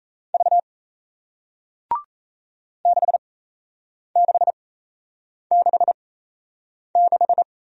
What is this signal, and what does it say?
Morse code, heard as a keyed tone, about 38 wpm, spaced as for 8 wpm, 700 Hz. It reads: V 6666